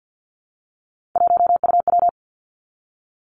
Morse code, heard as a keyed tone, about 50 wpm, 700 Hz. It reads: 1UW
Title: Three Challenging Letter-Number Combinations 50wpm